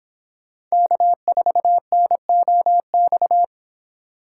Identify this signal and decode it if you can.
Morse code, heard as a keyed tone, about 26 wpm, 700 Hz. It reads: K4NOX